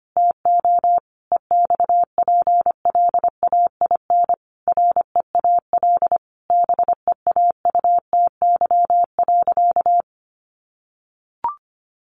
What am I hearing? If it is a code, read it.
Morse code, heard as a keyed tone, about 25 wpm, 700 Hz. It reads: TO EXPLAIN REAL BEAUTY.
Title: Sentences from Top 1100 Words 25wpm